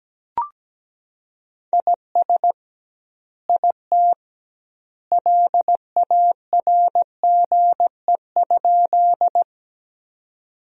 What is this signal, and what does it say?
Morse code, heard as a keyed tone, about 17 wpm, 700 Hz. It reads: IS IT LARGE?
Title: 2x Word-Spacing - Sentences from Top 200 Words - 17wpm